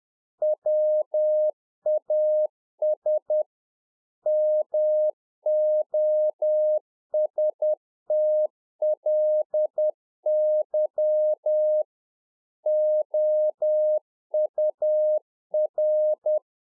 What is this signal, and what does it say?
Morse code, heard as a keyed tone, about 10 wpm, 610 Hz. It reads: WAS MOSTLY OUR